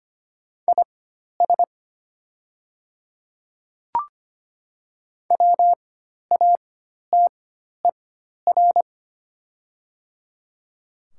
Morse code, heard as a keyed tone, about 25 wpm, spaced as for 12 wpm, 700 Hz. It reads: IS WATER
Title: 4x Character Spacing - Top 300 Words - 25wpm